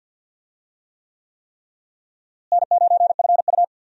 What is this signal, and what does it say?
Morse code, heard as a keyed tone, about 53 wpm, 700 Hz. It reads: N9FU
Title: USA Call Signs - 4 Characters - Repeated 3x - 50wpm